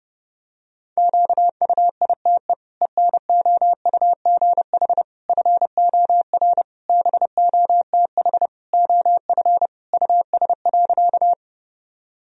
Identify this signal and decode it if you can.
Morse code, heard as a keyed tone, about 30 wpm, 700 Hz. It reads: QUITE ENOUGH FOR BOTH OF US.
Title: Sentences from Top 1000 Words - Review Part 1 - 30wpm